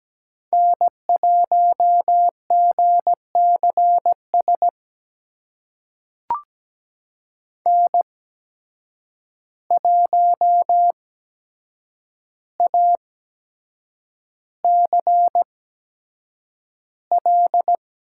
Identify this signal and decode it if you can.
Morse code, heard as a keyed tone, about 17 wpm, spaced as for 7 wpm, 700 Hz. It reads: N1GCS N1ACL
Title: USA Call Signs - 5 Characters - 8x Character Spacing - 17wpm